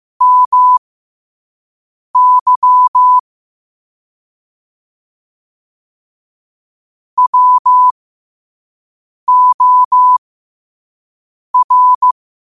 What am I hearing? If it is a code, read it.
Morse code, heard as a keyed tone, about 15 wpm, spaced as for 5 wpm, 1000 Hz. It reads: MY WOR
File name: qotdfiles / 5wpm